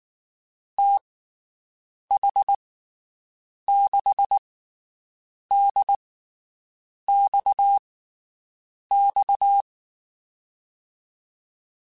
Morse code, heard as a keyed tone, about 19 wpm, spaced as for 7 wpm, 800 Hz. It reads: TH6DXX
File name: (spaced) / QSOs Spaced-mp3s / WD0BGP 1 @18 sp